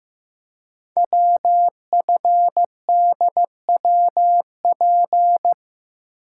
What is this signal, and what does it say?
Morse code, heard as a keyed tone, about 15 wpm, 700 Hz. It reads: WFDWP
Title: Code Groups - 5 Letters - 15wpm